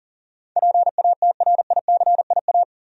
Morse code, heard as a keyed tone, about 38 wpm, 700 Hz. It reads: PATRICIA
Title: USA Common Names - 40wpm